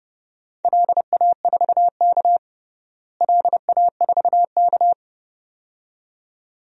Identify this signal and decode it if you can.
Morse code, heard as a keyed tone, about 30 wpm, 700 Hz. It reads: LA4K LA4K
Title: DXCC-100 Call Signs - 4 Characters - Repeated 2x - 30wpm